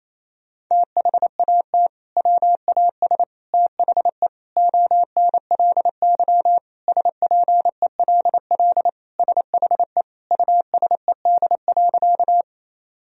Morse code, heard as a keyed tone, about 28 wpm, 700 Hz. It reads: THAT WAS THE ONLY SPELL SHE USED.
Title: Sentences from Top 1000 Words - Review Part 1 - 28wpm